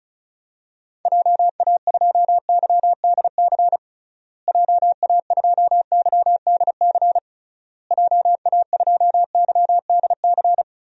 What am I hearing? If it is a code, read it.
Morse code, heard as a keyed tone, about 36 wpm, 700 Hz. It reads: JA2YDC JA2YDC JA2YDC
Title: DXCC-100 Call Signs - 6 Characters - Repeated 3x - 35wpm